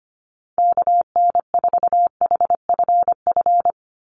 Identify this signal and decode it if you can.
Morse code, heard as a keyed tone, about 25 wpm, 700 Hz. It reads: KN4HFF
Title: USA Call Signs 25wpm